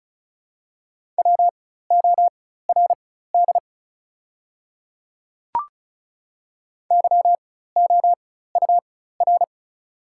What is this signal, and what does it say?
Morse code, heard as a keyed tone, about 36 wpm, spaced as for 16 wpm, 700 Hz. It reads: WORD YOUR
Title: 4x Character Spacing - Top 100 Words - 35wpm